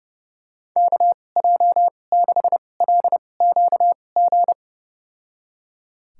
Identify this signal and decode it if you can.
Morse code, heard as a keyed tone, about 32 wpm, spaced as for 22 wpm, 700 Hz. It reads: KJ6LQG
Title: USA Call Signs - 6 Characters - 2x Character Spacing - 30wpm